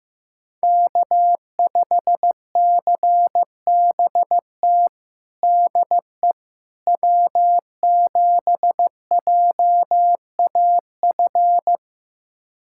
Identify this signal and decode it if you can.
Morse code, heard as a keyed tone, about 15 wpm, 700 Hz. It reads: K5CBT DE W7JAF
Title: USA Callsign DE Callsign - 5 Characters - 15wpm